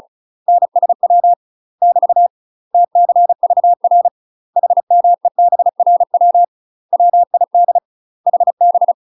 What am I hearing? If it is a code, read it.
Morse code, heard as a keyed tone, about 35 wpm, 700 Hz. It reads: NSW <BT> TCVR HMEBRW WID HB
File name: UK MASTER QSOs TRAINING / 35WPM / UK MASTER QSOs 0001